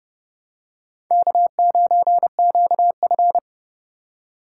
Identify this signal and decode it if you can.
Morse code, heard as a keyed tone, about 30 wpm, 700 Hz. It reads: K9QF
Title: USA Call Signs - 4 Characters - 4x Character Spacing - 30wpm